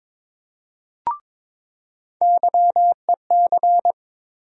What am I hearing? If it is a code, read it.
Morse code, heard as a keyed tone, about 22 wpm, 700 Hz. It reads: YEC
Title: USA Call Sign Postfixes - 3 Letters 22wpm